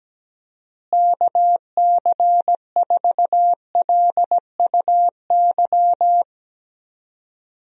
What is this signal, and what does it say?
Morse code, heard as a keyed tone, about 17 wpm, 700 Hz. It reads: KC4LUY
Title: USA Call Signs - 6 Characters - Repeated 3x - 17wpm